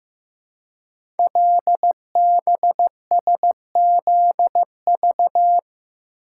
Morse code, heard as a keyed tone, about 15 wpm, 700 Hz. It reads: LBSZV
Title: Code Groups - 5 Letters - 15wpm